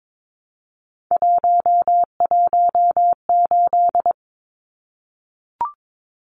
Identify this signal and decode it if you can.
Morse code, heard as a keyed tone, about 22 wpm, 700 Hz. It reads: 118
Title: Numbers - 3-Digit 22wpm